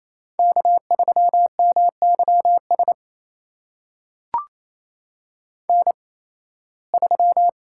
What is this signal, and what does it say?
Morse code, heard as a keyed tone, about 27 wpm, spaced as for 18 wpm, 700 Hz. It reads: K3MYS N3
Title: USA Call Signs - 5 Characters - 8x Character Spacing - 28wpm